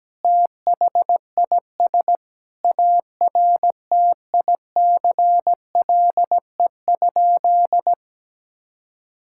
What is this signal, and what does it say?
Morse code, heard as a keyed tone, about 17 wpm, 700 Hz. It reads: THIS ARTICLE?
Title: Sentences from Top 1000 Words - Review Part 5 - 17wpm